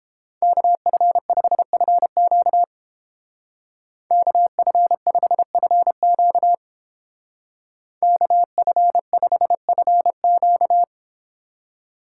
Morse code, heard as a keyed tone, about 30 wpm, 700 Hz. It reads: KF5FQ KF5FQ KF5FQ